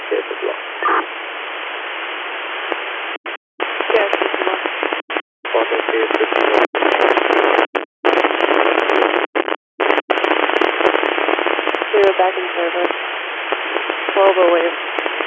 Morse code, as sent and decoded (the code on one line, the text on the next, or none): none